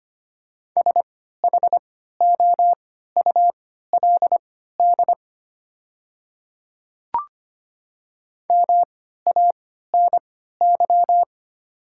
Morse code, heard as a keyed tone, about 25 wpm, spaced as for 14 wpm, 700 Hz. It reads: SHOULD MANY
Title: 3x Character Spacing - Top 300 Words - 25wpm